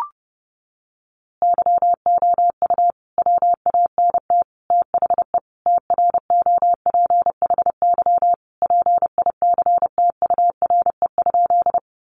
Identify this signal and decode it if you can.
Morse code, heard as a keyed tone, about 30 wpm, 700 Hz. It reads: YOU WANT THE TROPHY PICTURE?
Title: A Sound Of Thunder 30wpm